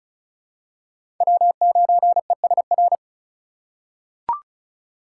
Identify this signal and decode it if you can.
Morse code, heard as a keyed tone, about 36 wpm, 700 Hz. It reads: W9ESR
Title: USA Call Signs - 5 Characters - 4x Character Spacing - 35wpm